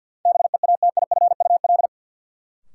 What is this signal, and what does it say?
Morse code, heard as a keyed tone, about 53 wpm, 700 Hz. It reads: BEATIFUL
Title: Sentences from Top 1300 Words - New Word Drill Alt - 50wpm